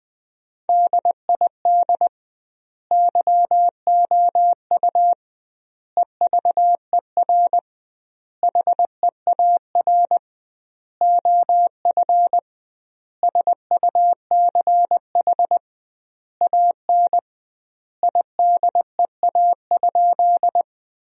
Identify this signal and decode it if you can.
Morse code, heard as a keyed tone, about 21 wpm, 700 Hz. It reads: DID YOU EVER HEAR OF SUCH AN IDEA?
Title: 2x Word-Spacing - Sentences from Top 400 Words - 20wpm